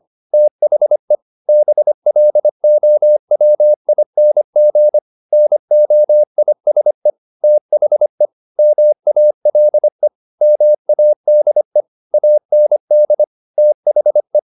Morse code, heard as a keyed tone, about 25 wpm, 600 Hz. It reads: THE BLOWING NOISE THE MALE MADE AND THE
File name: the old man and the sea part 1 25wpm